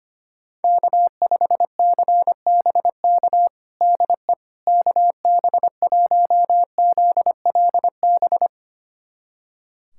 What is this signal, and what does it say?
Morse code, heard as a keyed tone, about 25 wpm, 700 Hz. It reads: K5CBK DE KB1ZLB
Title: USA Callsign DE Callsign 25wpm